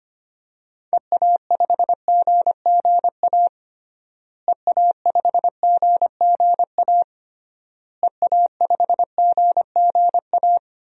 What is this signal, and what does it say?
Morse code, heard as a keyed tone, about 25 wpm, 700 Hz. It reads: EA5GGA EA5GGA EA5GGA
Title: DXCC-100 Call Signs - 6 Characters - Repeated 3x - 25wpm